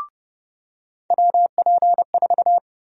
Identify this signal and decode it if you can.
Morse code, heard as a keyed tone, about 30 wpm, 700 Hz. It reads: WP4
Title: USA Call Sign Prefixes - 30wpm